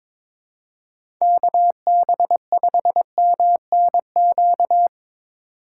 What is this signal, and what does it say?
Morse code, heard as a keyed tone, about 25 wpm, 700 Hz. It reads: KB5MNQ
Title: Speed Racing - USA 2x3 Call Signs 22wpm